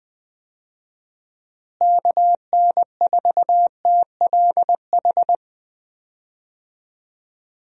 Morse code, heard as a keyed tone, about 20 wpm, 700 Hz. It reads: KN4TLH
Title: USA Call Signs - 6 Characters - Repeated 3x - 20wpm